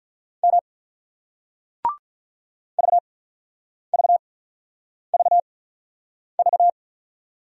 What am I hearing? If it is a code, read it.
Morse code, heard as a keyed tone, about 47 wpm, spaced as for 9 wpm, 700 Hz. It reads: M VVVV